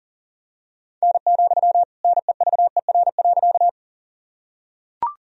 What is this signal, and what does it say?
Morse code, heard as a keyed tone, about 40 wpm, 700 Hz. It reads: N, NEVER.
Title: Sentences from Top 300 Words and Abbreviations 40wpm